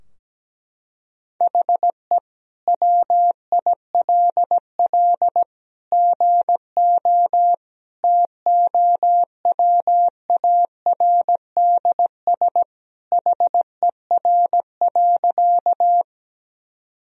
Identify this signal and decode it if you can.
Morse code, heard as a keyed tone, about 17 wpm, 700 Hz. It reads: HE WILL GO TOWARDS HER.
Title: Sentences from Top 1000 Words - Review Part 2 - 17wpm